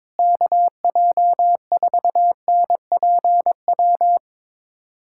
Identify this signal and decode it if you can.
Morse code, heard as a keyed tone, about 22 wpm, 700 Hz. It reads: KJ4NPW
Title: USA Callsign DE Callsign - 6 Characters - 22wpm